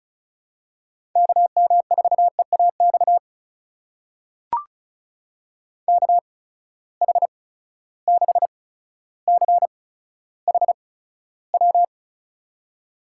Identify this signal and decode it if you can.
Morse code, heard as a keyed tone, about 34 wpm, spaced as for 15 wpm, 700 Hz. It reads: KM4EAX KH6CHW
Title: USA Call Signs - 6 Characters - 8x Character Spacing - 35wpm